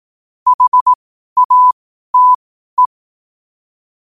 Morse code, heard as a keyed tone, about 18 wpm, spaced as for 12 wpm, 1000 Hz. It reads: HATE
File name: qotdfiles / 12wpm